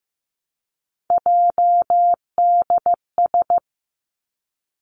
Morse code, heard as a keyed tone, about 15 wpm, 700 Hz. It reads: JDS
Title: Three Challenging Letter-Number Combinations 15wpm